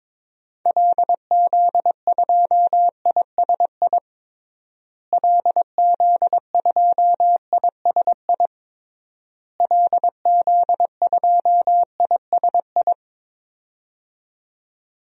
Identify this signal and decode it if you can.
Morse code, heard as a keyed tone, about 22 wpm, 700 Hz. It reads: LZ2ISI LZ2ISI LZ2ISI
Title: DXCC-100 Call Signs - 6 Characters - Repeated 3x - 22wpm